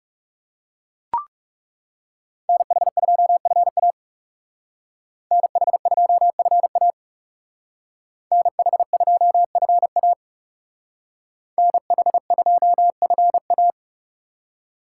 Morse code, heard as a keyed tone, about 39 wpm, 690 Hz. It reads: NH2FA NH2FA NH2FA NH2FA